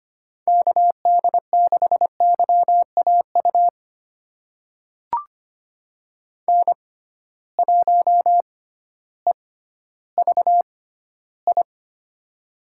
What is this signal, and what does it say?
Morse code, heard as a keyed tone, about 25 wpm, spaced as for 14 wpm, 700 Hz. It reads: KD6YAU N1EVI
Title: USA Call Signs - 6x Character Spacing - 25wpm